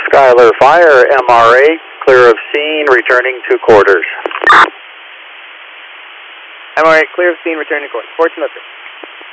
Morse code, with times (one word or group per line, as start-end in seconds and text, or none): none